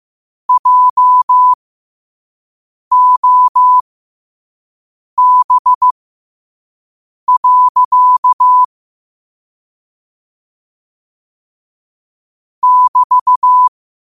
Morse code, heard as a keyed tone, about 15 wpm, spaced as for 5 wpm, 1000 Hz. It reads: JOB. <BT>
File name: qotdfiles / 5wpm